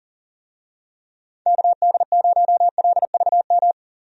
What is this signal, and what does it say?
Morse code, heard as a keyed tone, about 40 wpm, 700 Hz. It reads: KD0LVM
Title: USA Call Signs - 6 Characters - 40wpm